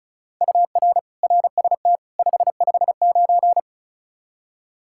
Morse code, heard as a keyed tone, about 36 wpm, 700 Hz. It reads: UR RST 559
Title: K1USN SST QSO Practice 35wpm